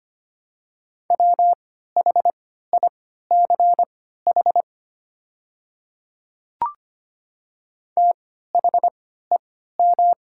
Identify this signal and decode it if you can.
Morse code, heard as a keyed tone, about 25 wpm, spaced as for 14 wpm, 690 Hz. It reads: WHICH THEM
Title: 3x Character Spacing - Top 300 Words - 25wpm